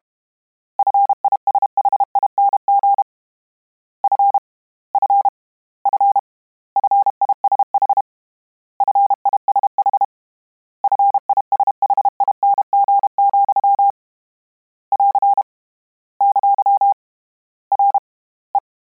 Morse code, heard as a keyed tone, about 32 wpm, 800 Hz. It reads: FISHING F F F FISH FISH FISHING, <AR> ! R E
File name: Combinations / Combinations-mp3s / Combinations@30